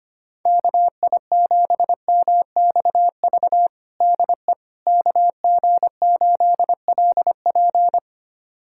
Morse code, heard as a keyed tone, about 25 wpm, 700 Hz. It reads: KI7MXV DE KG8LP
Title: USA Callsign DE Callsign 25wpm